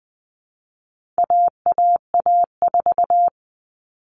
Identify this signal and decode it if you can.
Morse code, heard as a keyed tone, about 20 wpm, 700 Hz. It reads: AAA4
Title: All Cut Numbers - 4-Digit 20wpm